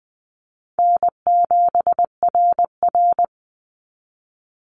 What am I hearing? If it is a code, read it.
Morse code, heard as a keyed tone, about 20 wpm, 700 Hz. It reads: N7RR
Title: NA SOTA Activator Callsigns 20wpm